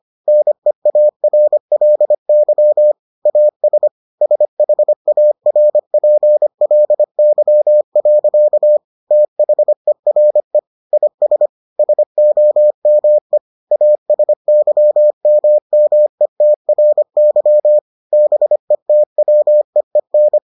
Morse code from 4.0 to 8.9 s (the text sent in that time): SHARPLY.